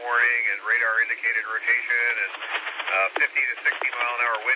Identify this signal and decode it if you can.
Emergency Alert System (SAME) tones, heard as harsh buzzing bursts, with speech. It is heard from 0.4 to 3.9 s.